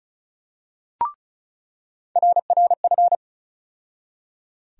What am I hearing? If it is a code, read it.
Morse code, heard as a keyed tone, about 34 wpm, 700 Hz. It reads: RRF
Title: USA Call Sign Postfixes - 3 Letters 35wpm